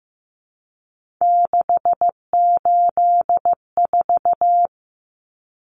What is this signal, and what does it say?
Morse code, heard as a keyed tone, about 15 wpm, 700 Hz. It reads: 684
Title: Numbers - 3-Digit 15wpm